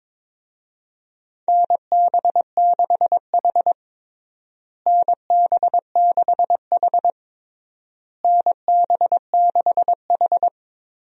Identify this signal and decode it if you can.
Morse code, heard as a keyed tone, about 22 wpm, 700 Hz. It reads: NB6H NB6H NB6H
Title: USA Call Signs - 4 Characters - Repeated 3x - 22wpm